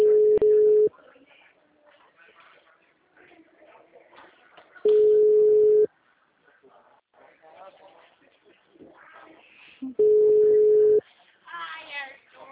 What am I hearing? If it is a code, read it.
Ringback tone (Europe, 425 Hz), with speech.